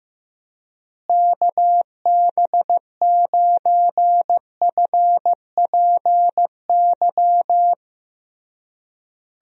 Morse code, heard as a keyed tone, about 15 wpm, 700 Hz. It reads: KB9FPY
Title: USA Call Signs - 6 Characters - Repeated 3x - 15wpm